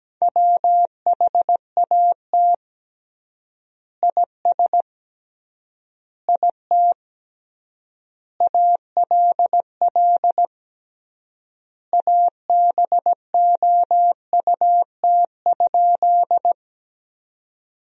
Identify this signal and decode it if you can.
Morse code, heard as a keyed tone, about 17 wpm, 700 Hz. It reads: WHAT IS IT ALL ABOUT?
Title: 3x Word-Spacing - Sentences from Top 100 Words - 17wpm